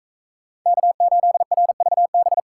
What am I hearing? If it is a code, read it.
Morse code, heard as a keyed tone, about 39 wpm, 700 Hz. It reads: K8RVB